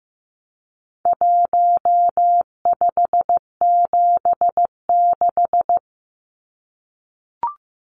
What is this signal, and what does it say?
Morse code, heard as a keyed tone, about 15 wpm, 700 Hz. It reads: 1576
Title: Numbers - 4-Digit 15wpm